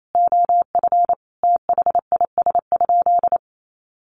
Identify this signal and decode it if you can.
Morse code, heard as a keyed tone, about 28 wpm, 700 Hz. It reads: OF THIS?
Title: Sentences from Top 900 Words 28wpm